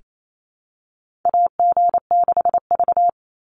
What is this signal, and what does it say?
Morse code, heard as a keyed tone, about 28 wpm, 700 Hz. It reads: AG6V
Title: Call Signs - 28wpm